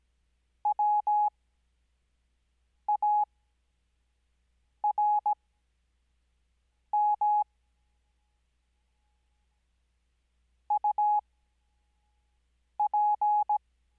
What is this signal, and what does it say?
Morse code, heard as a keyed tone, about 17 wpm, spaced as for 5 wpm, 840 Hz. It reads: WARM UP
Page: Farnsworth Spacing: 15WPM Character speed Spaced at 5WPM.